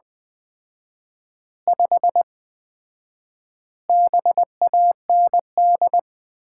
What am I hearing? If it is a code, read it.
Morse code, heard as a keyed tone, about 20 wpm, 700 Hz. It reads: 5 BAND
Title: Common QSO Phrases - 4x Word-Spacing - 20wpm